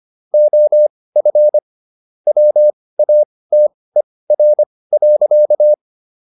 Morse code, heard as a keyed tone, about 25 wpm, spaced as for 18 wpm, 600 Hz. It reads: OF WATER.